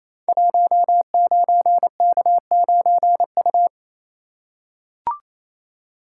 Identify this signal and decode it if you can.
Morse code, heard as a keyed tone, about 29 wpm, 700 Hz. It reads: 19K9U
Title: Code Groups - 5 Letter:Numbers - 28wpm